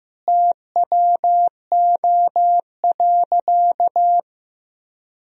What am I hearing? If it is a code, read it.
Morse code, heard as a keyed tone, about 15 wpm, 700 Hz. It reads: TWO.
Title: Sentences from Top 1000 Words - Review Part 1 - 15wpm